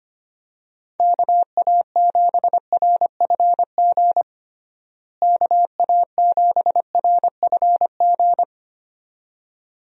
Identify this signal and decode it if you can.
Morse code, heard as a keyed tone, about 25 wpm, 700 Hz. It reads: KA7RFG KA7RFG
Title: USA Call Signs - 6 Characters - Repeated 2x - 25wpm